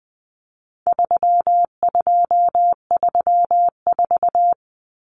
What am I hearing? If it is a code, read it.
Morse code, heard as a keyed tone, about 20 wpm, 700 Hz. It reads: 3234
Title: Common Cut Numbers - 4-Digit 20wpm